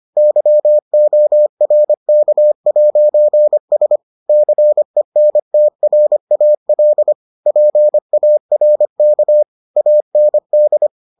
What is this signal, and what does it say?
Morse code, heard as a keyed tone, about 25 wpm, 600 Hz. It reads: YORK'S CENTRAL PARK AND